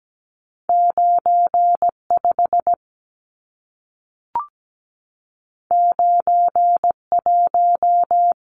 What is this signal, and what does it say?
Morse code, heard as a keyed tone, about 17 wpm, 700 Hz. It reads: 95 91
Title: Numbers - 2-Digit 17wpm